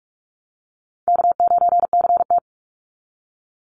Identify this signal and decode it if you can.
Morse code, heard as a keyed tone, about 44 wpm, 700 Hz. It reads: K9CT
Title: Easy Call Signs Repeated 3x - 45wpm